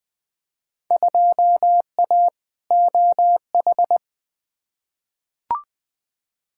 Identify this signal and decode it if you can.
Morse code, heard as a keyed tone, about 20 wpm, 700 Hz. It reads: 2A OH
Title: Field Day Exchanges - 20wpm